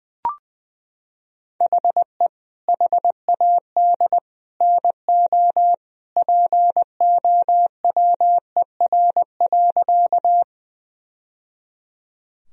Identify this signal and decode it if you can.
Morse code, heard as a keyed tone, about 20 wpm, 700 Hz. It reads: HE HAD NO POWER.
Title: Sentences from Top 1000 Words - Review Part 2 - 20wpm